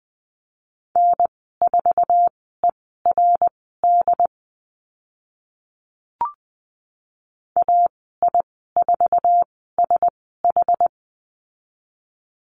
Morse code, heard as a keyed tone, about 20 wpm, spaced as for 14 wpm, 700 Hz. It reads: N4ERD AI4SH
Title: 069 Lesson 12g - Letter d - Call Signs 20wpm